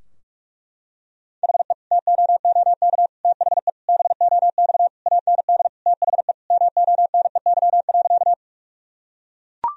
Morse code, heard as a keyed tone, about 42 wpm, 700 Hz. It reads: HE TOOK THE BOX AND THE MONEY.
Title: Sentences from Top 1000 Words - Review Part 2 - 45wpm